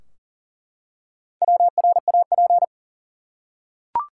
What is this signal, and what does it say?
Morse code, heard as a keyed tone, about 40 wpm, 700 Hz. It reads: WRAP
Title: Four Letter Words - 40wpm